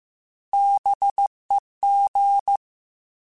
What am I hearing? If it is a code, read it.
Morse code, heard as a keyed tone, about 15 wpm, 800 Hz. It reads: BEG
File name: Alphabetical / Alphabetical mp3s / Bs @13